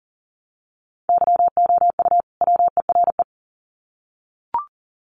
Morse code, heard as a keyed tone, about 40 wpm, 700 Hz. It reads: YOU WERE
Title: Sets of 2 Words 40wpm